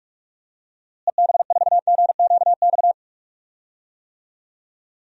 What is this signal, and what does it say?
Morse code, heard as a keyed tone, about 42 wpm, 700 Hz. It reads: EB4GQX
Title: DXCC-100 Call Signs - 6 Characters - Repeated 2x - 45wpm